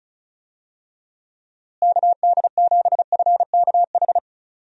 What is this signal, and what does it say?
Morse code, heard as a keyed tone, about 34 wpm, 700 Hz. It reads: KD7FKH
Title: USA Call Signs - 6 Characters - Repeated 2x - 35wpm